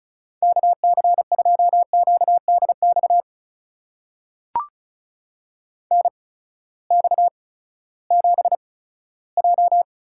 Morse code, heard as a keyed tone, about 36 wpm, spaced as for 18 wpm, 700 Hz. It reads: KC2QDX NX7J
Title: USA Call Signs - 8x Character Spacing - 35wpm